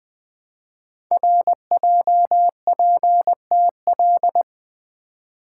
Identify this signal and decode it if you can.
Morse code, heard as a keyed tone, about 20 wpm, 700 Hz. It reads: RJPTL
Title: Code Groups - 5 Letters - 20wpm